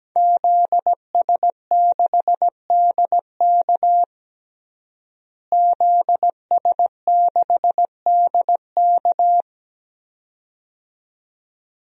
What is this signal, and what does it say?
Morse code, heard as a keyed tone, about 17 wpm, 700 Hz. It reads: ZS6DK ZS6DK
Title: DXCC-100 Call Signs - 5 Characters - Repeated 2x - 17wpm